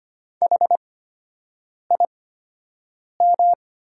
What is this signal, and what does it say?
Morse code, heard as a keyed tone, about 25 wpm, spaced as for 7 wpm, 700 Hz. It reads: HIM
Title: 8x Character Spacing - Top 100 Words - 25wpm